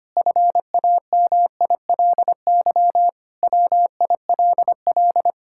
Morse code, heard as a keyed tone, about 25 wpm, 700 Hz. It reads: FAMILY WILL
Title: Sentences from Top 1000 Words - Review Part 5 - 25wpm